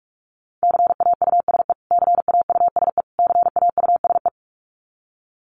Morse code, heard as a keyed tone, about 44 wpm, 700 Hz. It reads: CAUSE CAUSE CAUSE
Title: Top 300 Words Repeated 45wpm